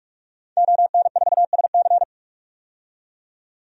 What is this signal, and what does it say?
Morse code, heard as a keyed tone, about 46 wpm, 700 Hz. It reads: ON4SC
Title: DXCC-100 Call Signs - 5 Characters - 45wpm